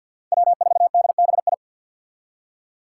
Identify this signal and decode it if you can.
Morse code, heard as a keyed tone, about 47 wpm, 700 Hz. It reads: W4DBI
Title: Five Challenging Letter-Number Combinations 50wpm